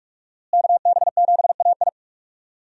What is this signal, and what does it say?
Morse code, heard as a keyed tone, about 44 wpm, 700 Hz. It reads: KB7AI
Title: USA Call Signs - 5 Characters - Repeated 3x - 45wpm